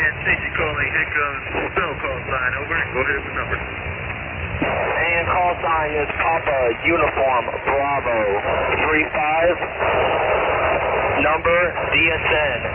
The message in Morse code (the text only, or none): none